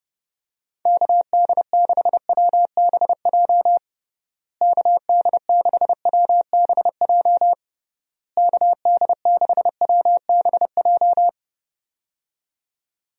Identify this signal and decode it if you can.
Morse code, heard as a keyed tone, about 30 wpm, 700 Hz. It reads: KD6WBJ KD6WBJ KD6WBJ
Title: USA Call Signs - 6 Characters - Repeated 3x - 30wpm